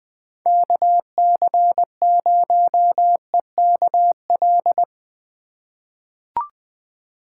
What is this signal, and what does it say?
Morse code, heard as a keyed tone, about 20 wpm, 700 Hz. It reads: KC0EKL
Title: USA Call Signs - 6 Characters - 4x Character Spacing - 20wpm